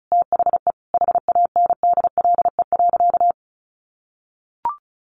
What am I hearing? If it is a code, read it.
Morse code, heard as a keyed tone, about 34 wpm, 700 Hz. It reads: THE HANDLE.
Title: Sentences from Top 1100 Words 35wpm